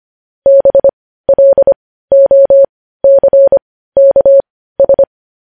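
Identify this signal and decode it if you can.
Morse code, heard as a keyed tone, about 25 wpm, spaced as for 15 wpm, 550 Hz. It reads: BLOCKS
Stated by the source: Morse practice sending news headlines at 15 wpm.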